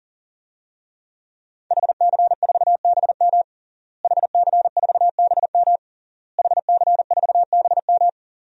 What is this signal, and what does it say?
Morse code, heard as a keyed tone, about 40 wpm, 700 Hz. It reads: HC4BM HC4BM HC4BM
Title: DXCC-100 Call Signs - 5 Characters - Repeated 3x - 40wpm